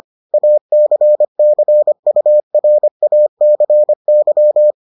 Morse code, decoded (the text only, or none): ACCURACY